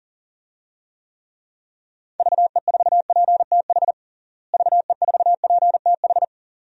Morse code, heard as a keyed tone, about 40 wpm, 700 Hz. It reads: VE4PTH VE4PTH
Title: DXCC-100 Call Signs - 6 Characters - Repeated 2x - 40wpm